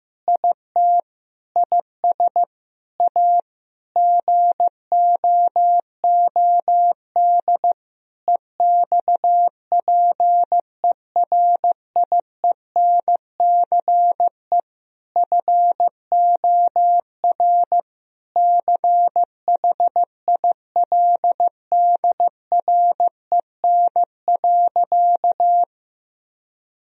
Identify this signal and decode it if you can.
Morse code, heard as a keyed tone, about 15 wpm, 700 Hz. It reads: IT IS A GOOD EXPERIENCE FOR CHILDREN.
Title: Sentences from Top 1000 Words - Review Part 4 - 15wpm